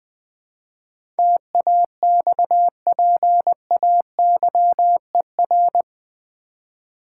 Morse code, heard as a keyed tone, about 20 wpm, 690 Hz. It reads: TAXPAYER
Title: 8 Letter Words - First 3 Letters Given - 20wpm